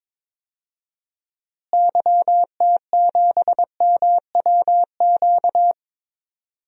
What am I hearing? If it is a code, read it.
Morse code, heard as a keyed tone, about 22 wpm, 700 Hz. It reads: YT7MWQ
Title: DXCC-100 Call Signs - 6 Characters - Repeated 3x - 22wpm